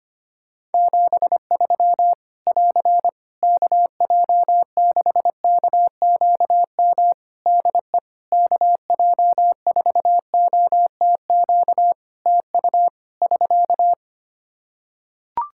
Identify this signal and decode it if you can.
Morse code, heard as a keyed tone, about 25 wpm, 700 Hz. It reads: 73 <AR> KJ6KQM DE KJ4OTQ TU <SK>
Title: Practice QSO - Exchange 4 - Ending - 25wpm